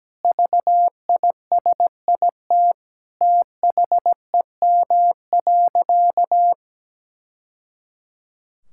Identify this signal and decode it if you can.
Morse code, heard as a keyed tone, about 17 wpm, 700 Hz. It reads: VISIT THEM.
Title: Sentences from Top 1000 Words - Review Part 6 - 17wpm